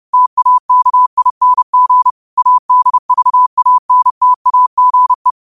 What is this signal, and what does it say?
Morse code, heard as a keyed tone, about 30 wpm, 1000 Hz. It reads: TAKING ADVANTAGE
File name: qotdfiles / 30wpm